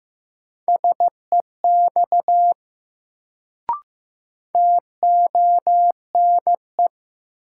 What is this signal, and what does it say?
Morse code, heard as a keyed tone, about 15 wpm, 700 Hz. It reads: SEX TONE
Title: Sentences from Top 1600 Words - New Word Drill - 15wpm